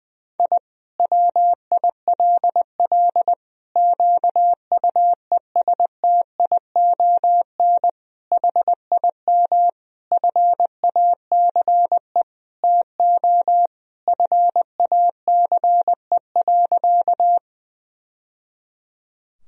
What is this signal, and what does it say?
Morse code, heard as a keyed tone, about 20 wpm, 700 Hz. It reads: I WILL QUESTION HIM FACE TO FACE.
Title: Sentences from Top 1000 Words - Review Part 2 - 20wpm